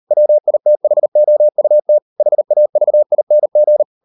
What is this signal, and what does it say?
Morse code, heard as a keyed tone, about 38 wpm, 600 Hz. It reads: WITHOUT HAVING